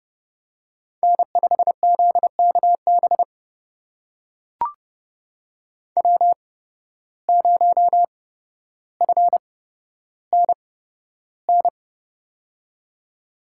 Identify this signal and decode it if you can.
Morse code, heard as a keyed tone, about 32 wpm, spaced as for 13 wpm, 700 Hz. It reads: N5ZKB W0FNN
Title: USA Call Signs - 5 Characters - 8x Character Spacing - 30wpm